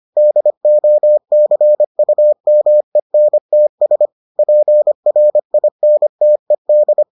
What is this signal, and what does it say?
Morse code, heard as a keyed tone, about 25 wpm, 600 Hz. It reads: DOCUMENTS PRINTED